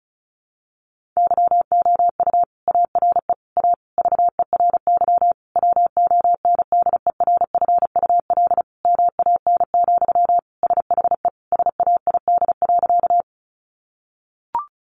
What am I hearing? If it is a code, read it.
Morse code, heard as a keyed tone, about 34 wpm, 700 Hz. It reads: YOU ARE A VERY WONDERFUL MAN, SHE SAID.
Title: Sentences from Top 1100 Words 35wpm